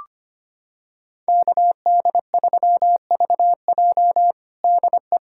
Morse code, heard as a keyed tone, about 25 wpm, 700 Hz. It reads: KD3VJ DE
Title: USA Callsign DE Callsign - 5 Characters - 25wpm